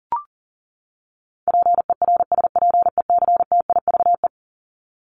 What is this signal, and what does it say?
Morse code, heard as a keyed tone, about 40 wpm, 700 Hz. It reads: PERSPECTIVE
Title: Top 1001-1250 Words 40wpm